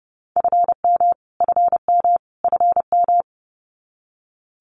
Morse code, heard as a keyed tone, about 30 wpm, 700 Hz. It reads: FM FM FM
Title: Abbreviations Repeated 3x - 30wpm